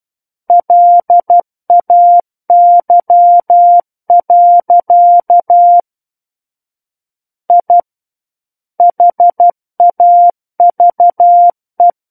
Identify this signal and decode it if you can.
Morse code, heard as a keyed tone, about 12 wpm, 700 Hz. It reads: LAY. I HAVE
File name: Book One / 03-Chapter 3 - 12 WPM